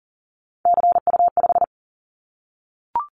Morse code, heard as a keyed tone, about 40 wpm, 700 Hz. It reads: CU5
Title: Callsign Prefixes for Top 100 DXCC - 40wpm